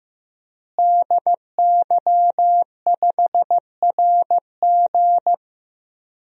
Morse code, heard as a keyed tone, about 16 wpm, 700 Hz. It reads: DY5RG
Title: DXCC-100 Call Signs - 5 Characters - Repeated 2x - 15wpm